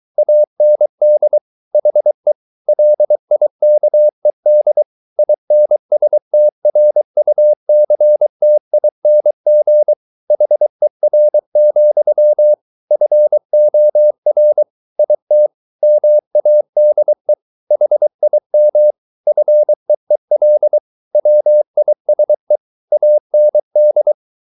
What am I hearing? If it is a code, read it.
Morse code, heard as a keyed tone, about 23 wpm, 600 Hz. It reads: AND HE LIKED INSTRUCTING HER, FOR IT MADE HIM FEEL WISE AND